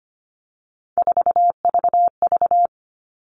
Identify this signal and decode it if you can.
Morse code, heard as a keyed tone, about 25 wpm, 700 Hz. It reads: 4VV
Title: All Cut Numbers - 3-Digit 25wpm